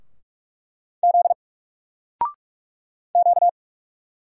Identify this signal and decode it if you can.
Morse code, heard as a keyed tone, about 46 wpm, 700 Hz. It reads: Z Q
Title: Single Letter-Number 45wpm